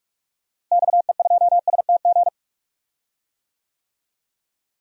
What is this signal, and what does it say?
Morse code, heard as a keyed tone, about 46 wpm, 700 Hz. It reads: XE2STG
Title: DXCC-100 Call Signs - 6 Characters - Repeated 2x - 45wpm